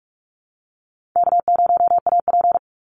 Morse code, heard as a keyed tone, about 44 wpm, 700 Hz. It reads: K0AP
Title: Call Signs Repeated 2x - 45wpm